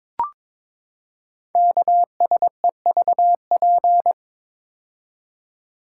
Morse code, heard as a keyed tone, about 22 wpm, 700 Hz. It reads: KSEVP
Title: Code Groups - 5 Letter:Numbers - 22wpm